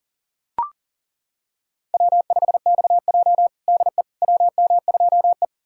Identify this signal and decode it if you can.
Morse code, heard as a keyed tone, about 37 wpm, 700 Hz. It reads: W5XJ DE WM2E
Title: USA Callsign DE Callsign - 4 Characters - 40wpm